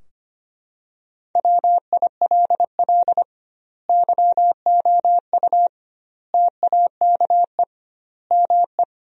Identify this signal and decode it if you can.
Morse code, heard as a keyed tone, about 25 wpm, 700 Hz. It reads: WILL YOU TAKE ME
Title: 2x Word-Spacing - Sentences from Top 200 Words - 25wpm